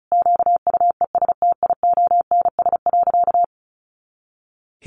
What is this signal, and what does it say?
Morse code, heard as a keyed tone, about 34 wpm, 700 Hz. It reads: QUESTIONS.
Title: Additional Easy Phrases 35wpm